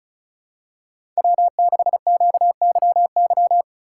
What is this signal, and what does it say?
Morse code, heard as a keyed tone, about 36 wpm, 700 Hz. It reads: W6QYY
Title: USA Call Signs - 3x Character Spacing - 35wpm